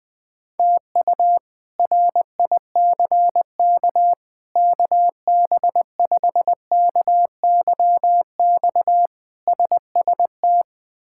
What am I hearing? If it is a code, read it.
Morse code, heard as a keyed tone, about 21 wpm, 700 Hz. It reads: TU RICK KB5KYX SST
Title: K1USN SST QSO Practice - 20wpm